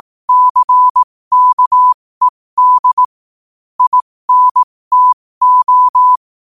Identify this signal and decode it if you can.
Morse code, heard as a keyed tone, about 18 wpm, spaced as for 15 wpm, 1000 Hz. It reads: CKED INTO
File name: qotdfiles / 15wpm